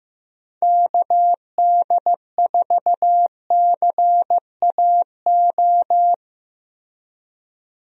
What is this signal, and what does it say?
Morse code, heard as a keyed tone, about 15 wpm, 700 Hz. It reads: KD4CAO
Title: USA Call Signs - 6 Characters - Repeated 2x - 15wpm